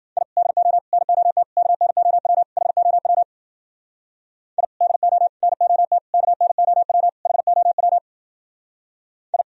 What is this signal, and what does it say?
Morse code, heard as a keyed tone, about 48 wpm, 700 Hz. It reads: I DO NOT KNOW HOW I DO NOT KNOW HOW I
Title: Speed-Racing - Sets of 5 Words 40wpm